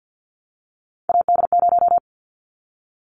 Morse code, heard as a keyed tone, about 50 wpm, 700 Hz. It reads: AD0
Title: Callsign Prefixes for Top 25 DXCC - 50wpm